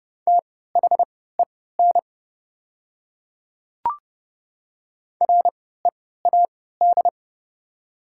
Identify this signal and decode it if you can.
Morse code, heard as a keyed tone, about 30 wpm, spaced as for 17 wpm, 690 Hz. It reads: THEN READ
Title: 3x Character Spacing - Top 200 Words - 30wpm